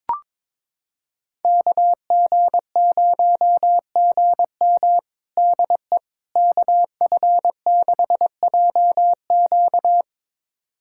Morse code, heard as a keyed tone, about 22 wpm, 700 Hz. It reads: KG0GM DE KF6JQ
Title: USA Callsign DE Callsign - 5 Characters - 22wpm